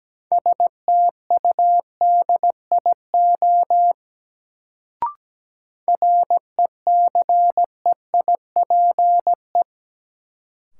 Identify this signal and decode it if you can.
Morse code, heard as a keyed tone, about 17 wpm, 700 Hz. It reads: STUDIO RECEIPE
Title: Sentences from Top 1200 Words - New Word Drill Alt - 17wpm